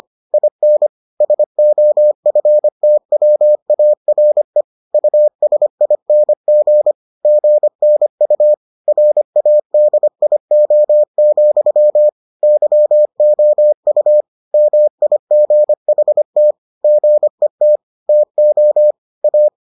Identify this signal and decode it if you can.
Morse code, heard as a keyed tone, about 25 wpm, 600 Hz. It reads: IN SOFTWARE USING GNU RADIO, YOU MIGHT GET TO A